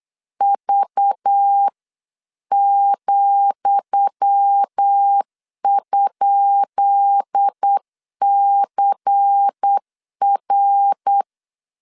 Morse code, heard as a keyed tone, about 8 wpm, 800 Hz. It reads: V,?CR
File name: LEZIONI / 10 SIGNS / 10D COMA 8wpm 3749